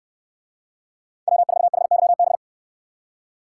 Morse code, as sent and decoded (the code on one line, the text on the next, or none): .-- ....- .... -.-- -...
W4HYB